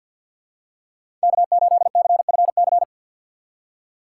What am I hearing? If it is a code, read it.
Morse code, heard as a keyed tone, about 49 wpm, 700 Hz. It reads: K8CFC
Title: USA Call Signs - 5 Characters - 2x Character Spacing - 50wpm